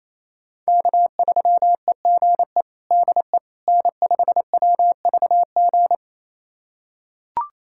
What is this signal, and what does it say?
Morse code, heard as a keyed tone, about 28 wpm, 700 Hz. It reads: K3EGE DE N5WVG
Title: USA Callsign DE Callsign - 5 Characters - 28wpm